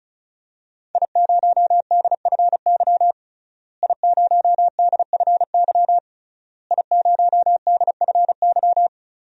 Morse code, heard as a keyed tone, about 34 wpm, 700 Hz. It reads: I0DFY I0DFY I0DFY
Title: DXCC-100 Call Signs - 5 Characters - Repeated 3x - 35wpm